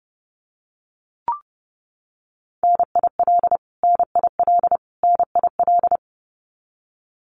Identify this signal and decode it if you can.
Morse code, heard as a keyed tone, about 30 wpm, 700 Hz. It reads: NIL NIL NIL
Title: Abbreviations Repeated 3x - 30wpm